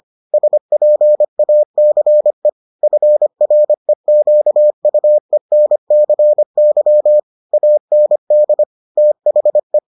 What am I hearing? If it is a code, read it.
Morse code, heard as a keyed tone, about 25 wpm, 600 Hz. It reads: SPACE FREQUENCY AND THE